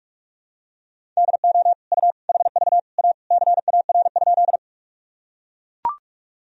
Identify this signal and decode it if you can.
Morse code, heard as a keyed tone, about 41 wpm, 700 Hz. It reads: DO U HV A CAR?
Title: Sentences from Top 300 Words and Abbreviations 45wpm